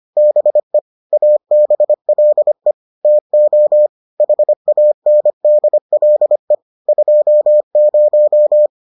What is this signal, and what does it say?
Morse code, heard as a keyed tone, about 25 wpm, 600 Hz. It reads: BE ABLE TO HANDLE 20